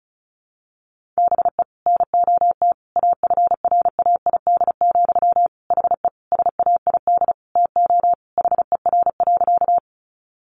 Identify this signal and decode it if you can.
Morse code, heard as a keyed tone, about 34 wpm, 700 Hz. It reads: BE NOT AFRAID, HE SAID TO HER.
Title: Sentences from Top 1500 Words 35wpm